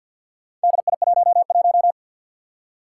Morse code, heard as a keyed tone, about 51 wpm, 700 Hz. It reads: DI11
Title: Four Challenging Letter-Number Combinations 50wpm